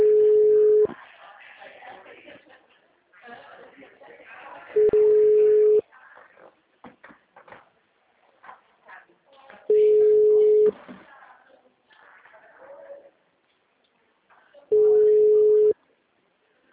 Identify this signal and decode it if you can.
Ringback tone (Europe, 425 Hz), with speech.
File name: out- -2003- - - .25092